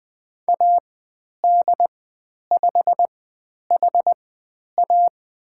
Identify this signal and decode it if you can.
Morse code, heard as a keyed tone, about 20 wpm, spaced as for 10 wpm, 700 Hz. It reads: AD5HA
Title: 102 Lesson 17g - Letter u - Call Signs 20wpm